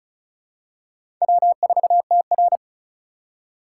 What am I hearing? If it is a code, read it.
Morse code, heard as a keyed tone, about 36 wpm, 700 Hz. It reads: W4TR
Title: USA Call Signs - 8x Character Spacing - 35wpm